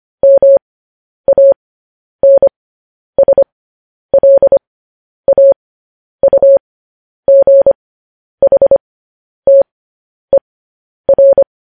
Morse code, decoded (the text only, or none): MANSLAUGHTER